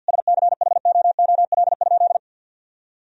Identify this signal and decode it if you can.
Morse code, heard as a keyed tone, about 50 wpm, 700 Hz. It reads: SCHOOL?
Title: Sentences from Top 1000 Words - Review Part 1 - 50wpm